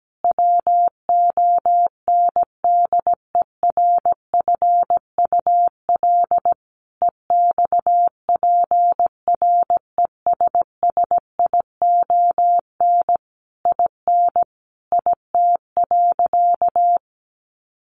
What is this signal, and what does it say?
Morse code, heard as a keyed tone, about 17 wpm, 700 Hz. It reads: WONDERFUL EXPRESSION IN IT.
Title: Sentences from Top 1200 Words 17wpm